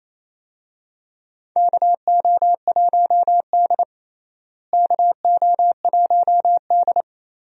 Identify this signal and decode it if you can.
Morse code, heard as a keyed tone, about 28 wpm, 700 Hz. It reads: KO1D KO1D
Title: USA Call Signs - 4 Characters - Repeated 2x - 28wpm